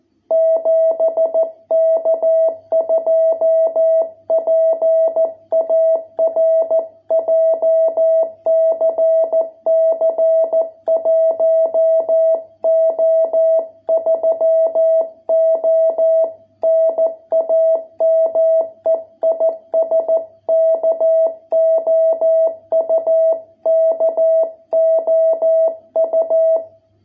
Morse code, heard as a keyed tone, about 14 wpm, 660 Hz. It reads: 7K2PAR JCC 1O3O NAME IS KOUKOU